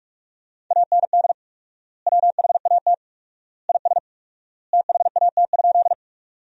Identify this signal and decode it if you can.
Morse code, heard as a keyed tone, about 43 wpm, 700 Hz. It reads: AND WHAT IS THAT?
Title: 4x Word-Spacing - Sentences from Top 100 Words - 45wpm